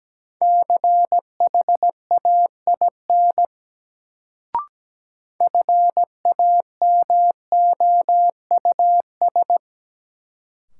Morse code, heard as a keyed tone, about 17 wpm, 700 Hz. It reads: CHAIN FAMOUS
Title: Sentences from Top 1500 Words - New Word Drill - 17wpm